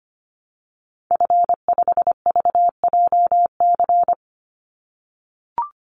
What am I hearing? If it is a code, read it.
Morse code, heard as a keyed tone, about 25 wpm, 700 Hz. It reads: F5VJC
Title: Call Signs - 25wpm